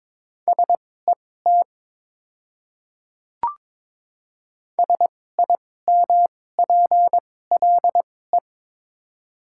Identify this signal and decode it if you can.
Morse code, heard as a keyed tone, about 22 wpm, spaced as for 16 wpm, 690 Hz. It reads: SET SIMPLE
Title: 2x Character Spacing - Top 500 Words - 22wpm